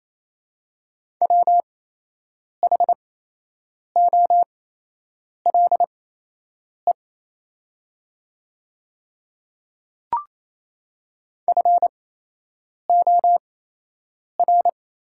Morse code, heard as a keyed tone, about 28 wpm, spaced as for 8 wpm, 700 Hz. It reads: WHOLE FOR
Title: 8x Character Spacing - Top 500 Words - 28wpm